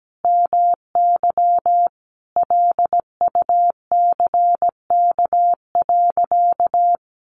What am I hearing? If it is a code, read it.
Morse code, heard as a keyed tone, about 17 wpm, 700 Hz. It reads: MY LUCK.
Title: Sentences from Top 1500 Words 17wpm